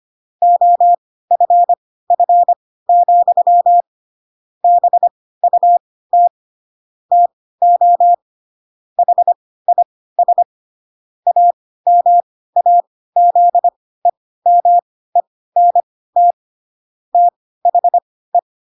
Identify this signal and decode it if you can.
Morse code, heard as a keyed tone, about 25 wpm, spaced as for 16 wpm, 700 Hz. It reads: OFF, BUT TO HIS AMAZEMENT THE
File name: Pinocchio / Pinocchio ch5 10 25 ik1hge